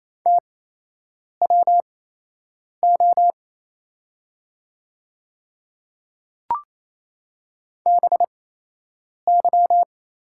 Morse code, heard as a keyed tone, about 28 wpm, spaced as for 8 wpm, 690 Hz. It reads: TWO BY
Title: 8x Character Spacing - Top 100 Words - 28wpm